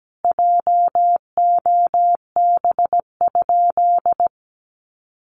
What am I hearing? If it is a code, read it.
Morse code, heard as a keyed tone, about 17 wpm, 700 Hz. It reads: JOB?
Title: Sentences from Top 1300 Words 17wpm